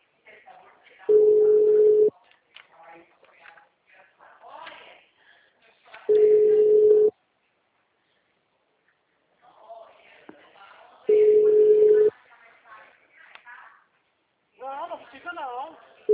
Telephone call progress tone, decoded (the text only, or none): ringback tone